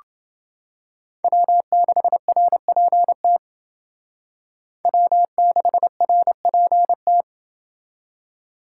Morse code, heard as a keyed tone, about 28 wpm, 700 Hz. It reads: W6RPT W6RPT